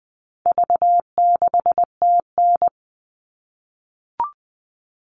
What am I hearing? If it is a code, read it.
Morse code, heard as a keyed tone, about 20 wpm, 700 Hz. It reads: V6TN
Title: All Cut Numbers - 4-Digit 20wpm